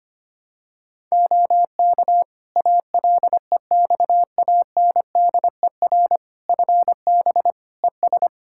Morse code, heard as a keyed tone, about 25 wpm, 700 Hz. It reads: OK ALEXANDER FB ES
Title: Practice QSO - Exchange 2 - Info - 25wpm